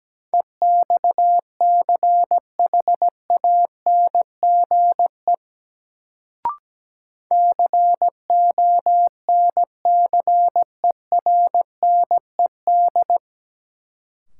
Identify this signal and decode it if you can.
Morse code, heard as a keyed tone, about 17 wpm, 700 Hz. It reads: EXCHANGE CONCERNED
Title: Sentences from Top 1000 Words - Review Part 6 - New Word Drill - 17wpm